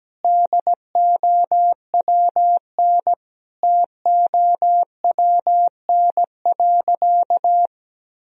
Morse code, heard as a keyed tone, about 17 wpm, 700 Hz. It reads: DOWN TOWN.
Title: Sentences from Top 1000 Words - Review Part 4 - 17wpm